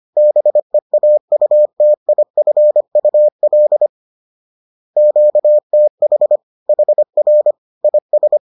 Morse code, heard as a keyed tone, about 25 wpm, 600 Hz. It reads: BEAUTIFUL QTH HR IS
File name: 25wpm-6